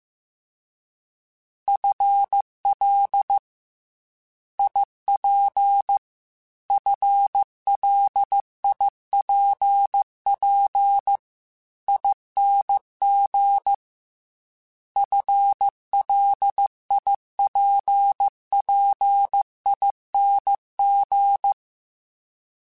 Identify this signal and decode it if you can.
Morse code, heard as a keyed tone, about 15 wpm, 800 Hz. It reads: FL IP FLIPP ING FLIPPING
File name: FL / FL mp3s / Fast FL @13